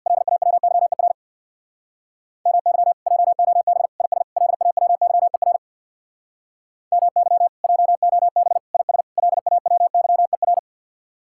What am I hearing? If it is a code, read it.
Morse code, heard as a keyed tone, about 65 wpm, 700 Hz. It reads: LAWYER MY JOB IS LAWYER MY JOB IS LAWYER